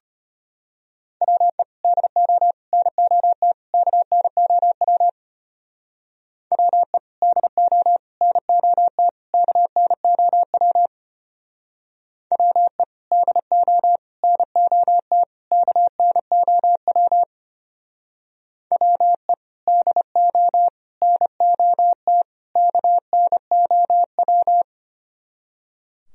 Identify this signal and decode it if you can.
Morse code, heard as a keyed tone, about 33 wpm, 700 Hz. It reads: WE DO NOT KNOW WE DO NOT KNOW WE DO NOT KNOW WE DO NOT KNOW